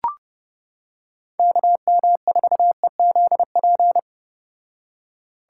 Morse code, heard as a keyed tone, about 30 wpm, 700 Hz. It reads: KM4EZP